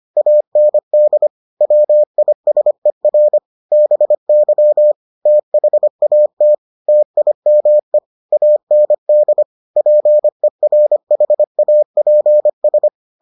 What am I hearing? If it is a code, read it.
Morse code, heard as a keyed tone, about 25 wpm, 600 Hz. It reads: AND WISER BY THAT TIME AND PERHAPS